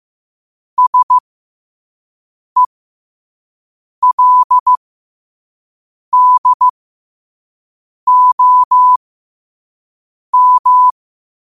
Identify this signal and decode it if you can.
Morse code, heard as a keyed tone, about 15 wpm, spaced as for 5 wpm, 1000 Hz. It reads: SELDOM
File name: qotdfiles / 5wpm